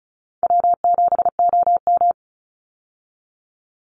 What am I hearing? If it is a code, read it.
Morse code, heard as a keyed tone, about 34 wpm, 700 Hz. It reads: W7OM
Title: Easy Call Signs Repeated 2x - 35wpm